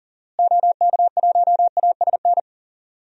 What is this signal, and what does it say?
Morse code, heard as a keyed tone, about 37 wpm, 700 Hz. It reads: OK1ASN
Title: DXCC-100 Call Signs - 6 Characters - Repeated 3x - 40wpm